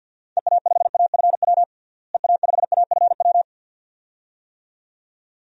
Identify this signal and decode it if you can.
Morse code, heard as a keyed tone, about 52 wpm, 700 Hz. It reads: EA5AFW EA5AFW
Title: DXCC-100 Call Signs - 6 Characters - Repeated 2x - 50wpm